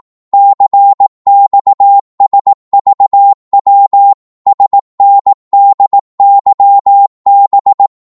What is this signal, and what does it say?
Morse code, heard as a keyed tone, about 18 wpm, 800 Hz. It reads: CXSVW SNDYB
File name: Morse / ltr-18wpm